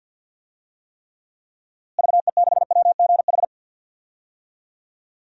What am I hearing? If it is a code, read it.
Morse code, heard as a keyed tone, about 52 wpm, 700 Hz. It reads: VE6WGH
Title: DXCC-100 Call Signs - 6 Characters - Repeated 2x - 50wpm